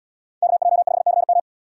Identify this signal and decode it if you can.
Morse code, heard as a keyed tone, about 77 wpm, 700 Hz. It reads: KJ5CM